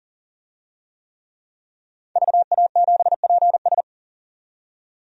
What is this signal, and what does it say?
Morse code, heard as a keyed tone, about 41 wpm, 700 Hz. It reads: VA7PS
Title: DXCC-100 Call Signs - 5 Characters - 40wpm